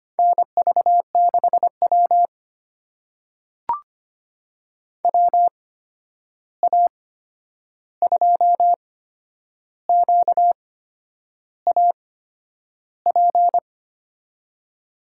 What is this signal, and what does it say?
Morse code, heard as a keyed tone, about 25 wpm, spaced as for 9 wpm, 700 Hz. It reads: NV6W WA2QAP
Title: USA Call Signs - 8x Character Spacing - 25wpm